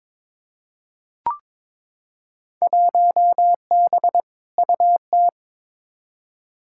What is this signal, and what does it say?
Morse code, heard as a keyed tone, about 22 wpm, 700 Hz. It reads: 1B UT
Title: Field Day Exchanges - 22wpm